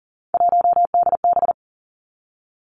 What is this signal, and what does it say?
Morse code, heard as a keyed tone, about 38 wpm, 700 Hz. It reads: 1DB
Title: Three Challenging Letter-Number Combinations 40wpm